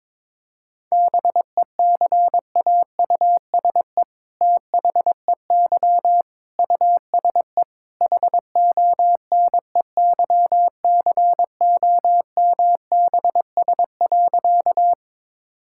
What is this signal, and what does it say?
Morse code, heard as a keyed tone, about 22 wpm, 700 Hz. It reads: BECAUSE THEY USE HONEYCOMBS.
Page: morse-code-ninja-practice-22wpm